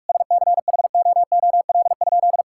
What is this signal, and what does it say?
Morse code, heard as a keyed tone, about 46 wpm, 700 Hz. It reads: SCHOOL?
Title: Sentences from Top 1000 Words - Review Part 5 - 45wpm